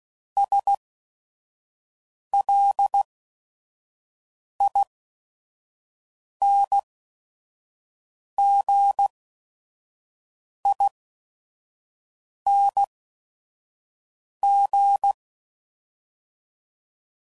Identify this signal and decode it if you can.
Morse code, heard as a keyed tone, about 16 wpm, spaced as for 5 wpm, 800 Hz. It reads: SLINGING